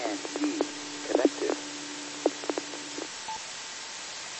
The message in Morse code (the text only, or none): none